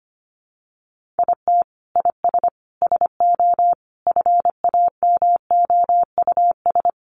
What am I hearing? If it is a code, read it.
Morse code, heard as a keyed tone, about 25 wpm, 700 Hz. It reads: IT IS SO FAMOUS
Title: Sentences from Top 1500 Words 25wpm